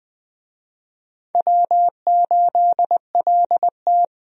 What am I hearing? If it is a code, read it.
Morse code, heard as a keyed tone, about 20 wpm, 700 Hz. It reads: W8LT
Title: USA Call Signs - 4 Characters - Repeated 3x - 20wpm